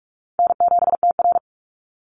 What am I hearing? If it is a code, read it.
Morse code, heard as a keyed tone, about 46 wpm, 700 Hz. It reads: N7TR
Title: Easy Call Signs - 45wpm